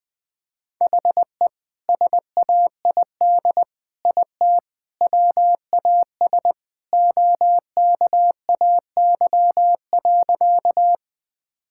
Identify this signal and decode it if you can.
Morse code, heard as a keyed tone, about 21 wpm, 700 Hz. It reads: HE SAID IT WAS OKAY.
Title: Sentences with 2-Word Phrases - 20wpm